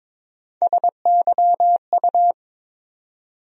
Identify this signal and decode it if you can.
Morse code, heard as a keyed tone, about 21 wpm, 700 Hz. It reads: SYU
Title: USA Call Sign Postfixes - 3 Letters 22wpm